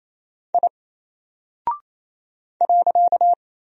Morse code, heard as a keyed tone, about 28 wpm, 700 Hz. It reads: I .
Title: Single Character (Letter:Number:Punctuation:Prosign) Alt 28wpm